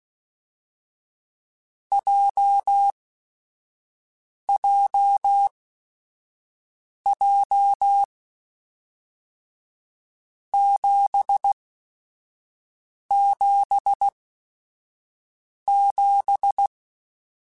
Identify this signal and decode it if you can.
Morse code, heard as a keyed tone, about 16 wpm, spaced as for 5 wpm, 800 Hz. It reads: JJJ 777
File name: Groups@5 / Groups@5 mp3s / JZ17@5